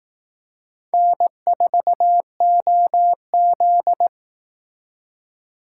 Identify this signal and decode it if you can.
Morse code, heard as a keyed tone, about 18 wpm, 700 Hz. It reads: N4OZ
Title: Speed Racing - USA 1x2 Call Signs 15wpm